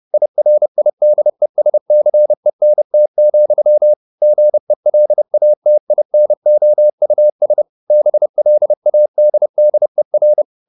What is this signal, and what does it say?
Morse code, heard as a keyed tone, about 30 wpm, 600 Hz. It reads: IRIDESCENT, GELATINOUS BLADDER
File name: the old man and the sea / the old man and the sea 1 30wpm